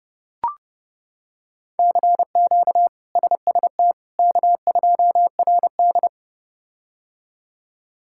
Morse code, heard as a keyed tone, about 30 wpm, 700 Hz. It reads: CQ SST K2RD
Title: K1USN SST QSO Practice - 30wpm